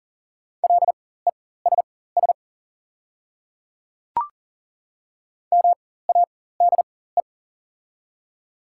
Morse code, heard as a keyed tone, about 39 wpm, spaced as for 19 wpm, 690 Hz. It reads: LESS MADE
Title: 4x Character Spacing - Top 500 Words - 40wpm